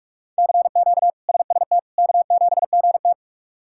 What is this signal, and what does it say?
Morse code, heard as a keyed tone, about 42 wpm, 700 Hz. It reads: CQ SST K7GT
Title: K1USN SST QSO Practice - 45wpm